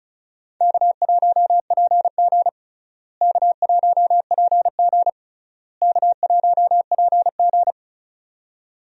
Morse code, heard as a keyed tone, about 36 wpm, 700 Hz. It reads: K1PG K1PG K1PG
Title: USA Call Signs - 4 Characters - Repeated 3x - 35wpm